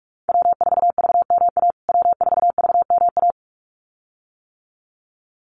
Morse code, heard as a keyed tone, about 44 wpm, 690 Hz. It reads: W4VMA W4VMA
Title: USA Call Signs Repeated 2x 45wpm